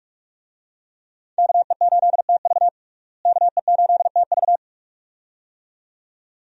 Morse code, heard as a keyed tone, about 46 wpm, 700 Hz. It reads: KE8TV KE8TV
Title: USA Call Signs - 5 Characters - Repeated 2x - 45wpm